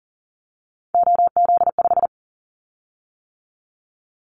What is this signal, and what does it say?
Morse code, heard as a keyed tone, about 40 wpm, 700 Hz. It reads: OZ5
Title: Callsign Prefixes for Top 50 DXCC - 40wpm